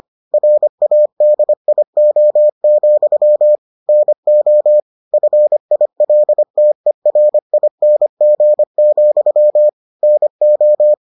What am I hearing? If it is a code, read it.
Morse code, heard as a keyed tone, about 25 wpm, 600 Hz. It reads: RADIO, NO FILTERING, NO